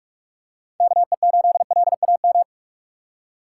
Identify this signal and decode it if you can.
Morse code, heard as a keyed tone, about 46 wpm, 700 Hz. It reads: KE8LAM
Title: USA Call Signs - 6 Characters - 2x Character Spacing - 45wpm